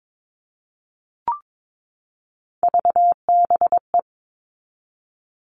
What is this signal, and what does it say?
Morse code, heard as a keyed tone, about 22 wpm, 700 Hz. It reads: VBE
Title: All Cut Numbers - 3-Digit 22wpm